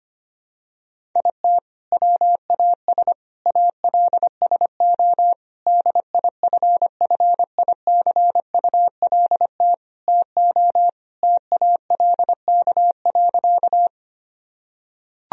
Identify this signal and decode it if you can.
Morse code, heard as a keyed tone, about 25 wpm, 700 Hz. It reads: IT WAS ALSO DIFFICULT TO TALK.
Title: Sentences from Top 1000 Words - Review Part 1 - 25wpm